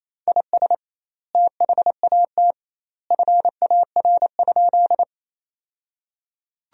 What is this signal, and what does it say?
Morse code, heard as a keyed tone, about 28 wpm, 700 Hz. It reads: IS THAT FAR?
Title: 2x Word-Spacing - Sentences from Top 300 Words - 28wpm